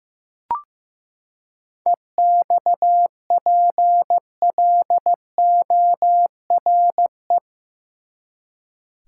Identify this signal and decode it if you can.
Morse code, heard as a keyed tone, about 15 wpm, 700 Hz. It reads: EXPLORE
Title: Seven Letter Words - 15wpm